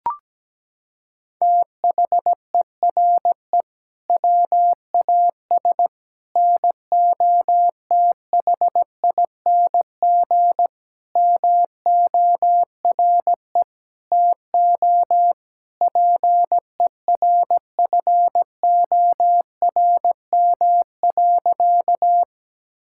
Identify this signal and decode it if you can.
Morse code, heard as a keyed tone, about 17 wpm, 700 Hz. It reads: THERE WAS NOTHING MORE TO PERFORM.
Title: Sentences from Top 1000 Words - Review Part 5 - 17wpm